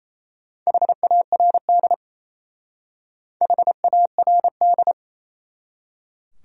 Morse code, heard as a keyed tone, about 35 wpm, 700 Hz. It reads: HARD HARD
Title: Speed-Racing Top 500 Words 28wpm